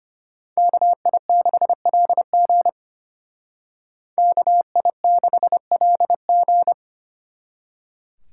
Morse code, heard as a keyed tone, about 32 wpm, 700 Hz. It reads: KI6LG KI6LG